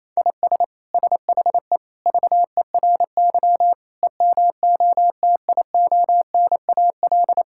Morse code, heard as a keyed tone, about 28 wpm, 700 Hz. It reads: IS SHE VERY EMOTIONAL
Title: Sentences from Top 1000 Words - Review Part 5 - 28wpm